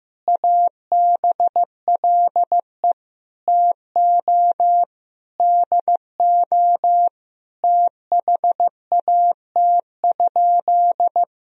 Morse code, heard as a keyed tone, about 15 wpm, 700 Hz. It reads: ABLE TO DO THAT?
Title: Sentences from Top 1000 Words - Review Part 2 - 15wpm